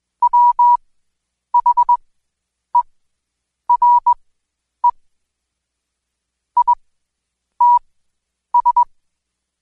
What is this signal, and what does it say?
Morse code, heard as a keyed tone, about 21 wpm, spaced as for 9 wpm, 970 Hz. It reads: WHERE ITS
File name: Code Quick / AudioDownld2 / Seg116-Day21-DayDrill 21